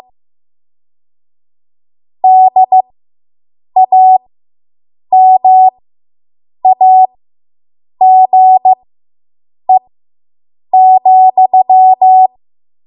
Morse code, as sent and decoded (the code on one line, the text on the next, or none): -.. .- -- .- --. --..--
DAMAGE,